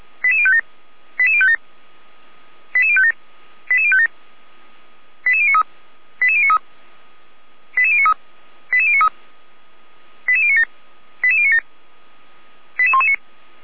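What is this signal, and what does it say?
Five-tone selective call (ZVEI), heard as quick runs of five tones, tones of about 70 ms. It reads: 88057 88057 88057 88057 88093 88093 88093 88093 88097 88097 88119
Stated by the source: ZVEI five-tone selective call 88055 received on an SDR.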